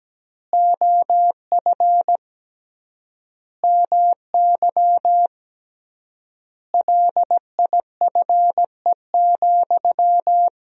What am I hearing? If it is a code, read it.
Morse code, heard as a keyed tone, about 17 wpm, 700 Hz. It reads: OF MY LIFE,
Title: 3x Word-Spacing - Sentences from Top 300 Words - 17wpm